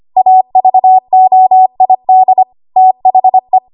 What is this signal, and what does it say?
Morse code, heard as a keyed tone, about 25 wpm, 750 Hz. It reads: AVOID THE
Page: W1AW Web Code Practice Files